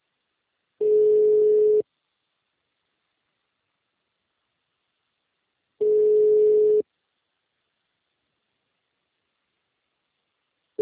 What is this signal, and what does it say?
Ringback tone (Europe, 425 Hz).
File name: out- -2006- - - .62954